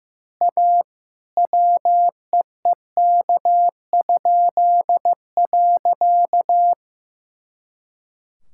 Morse code, heard as a keyed tone, about 15 wpm, 700 Hz. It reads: A WEEK?.
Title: Pangram Challenge v2 - 15wpm